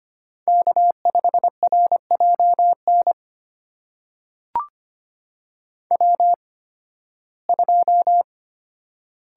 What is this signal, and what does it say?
Morse code, heard as a keyed tone, about 25 wpm, spaced as for 16 wpm, 700 Hz. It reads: K5RJN W2
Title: USA Call Signs - 5 Characters - 8x Character Spacing - 25wpm